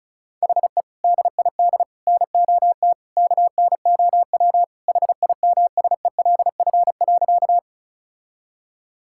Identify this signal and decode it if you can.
Morse code, heard as a keyed tone, about 36 wpm, 700 Hz. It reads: HE DID NOT KNOW HIMSELF.
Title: Sentences from Top 1600 Words 35wpm